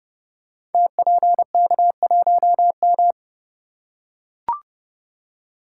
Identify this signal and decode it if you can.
Morse code, heard as a keyed tone, about 30 wpm, 700 Hz. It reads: TPK1M
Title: Code Groups - 5 Letter:Numbers - 30wpm